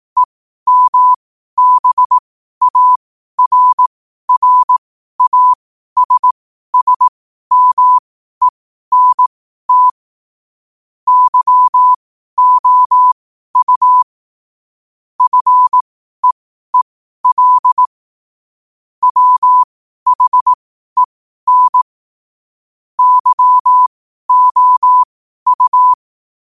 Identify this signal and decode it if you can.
Morse code, heard as a keyed tone, about 18 wpm, spaced as for 12 wpm, 1000 Hz. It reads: EMBARRASSMENT YOU FEEL WHEN YOU
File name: qotdfiles / 12wpm